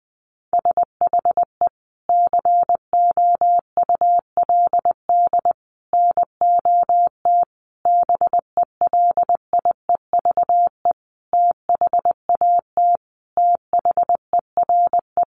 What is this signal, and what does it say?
Morse code, heard as a keyed tone, about 20 wpm, 700 Hz. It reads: SHE COULD NOT BELIEVE THAT THERE
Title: Sentences from Top 1000 Words 20wpm